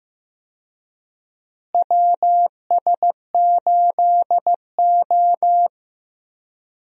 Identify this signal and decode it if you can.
Morse code, heard as a keyed tone, about 16 wpm, 700 Hz. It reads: WS8O
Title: USA Call Signs - 4 Characters - Repeated 2x - 15wpm